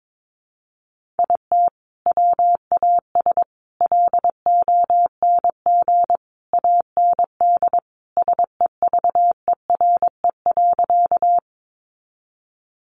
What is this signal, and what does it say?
Morse code, heard as a keyed tone, about 22 wpm, 700 Hz. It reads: IT WAS LONG AND SEVERE.
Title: Sentences from Top 1400 Words 22wpm